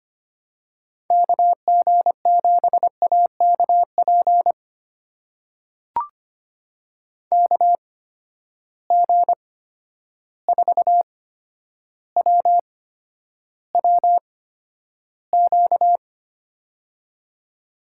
Morse code, heard as a keyed tone, about 25 wpm, spaced as for 11 wpm, 700 Hz. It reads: KG7AKP KG4WWQ
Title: USA Call Signs - 6 Characters - 8x Character Spacing - 25wpm